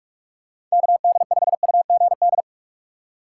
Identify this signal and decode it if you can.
Morse code, heard as a keyed tone, about 41 wpm, 700 Hz. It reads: KD5UGD
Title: USA Call Signs - 6 Characters - 6x Character Spacing - 45wpm